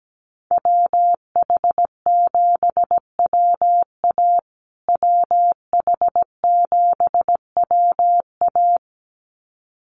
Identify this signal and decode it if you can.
Morse code, heard as a keyed tone, about 17 wpm, 700 Hz. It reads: WH7WA WH7WA
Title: USA Call Signs Repeated 2x 17wpm